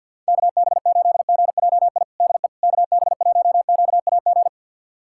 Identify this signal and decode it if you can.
Morse code, heard as a keyed tone, about 52 wpm, 700 Hz. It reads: KB8GJI DE KB1QAG
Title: USA Callsign DE Callsign 50wpm